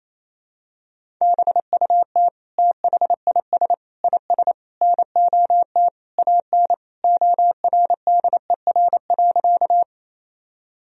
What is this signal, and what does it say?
Morse code, heard as a keyed tone, about 28 wpm, 700 Hz. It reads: BUT THIS IS NOT AN ORDER.
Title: Sentences from Top 1000 Words - Review Part 2 - 28wpm